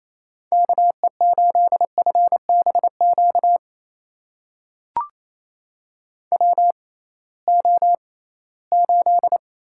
Morse code, heard as a keyed tone, about 28 wpm, spaced as for 18 wpm, 700 Hz. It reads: KE8FBQ WO8
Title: USA Call Signs - 6x Character Spacing - 28wpm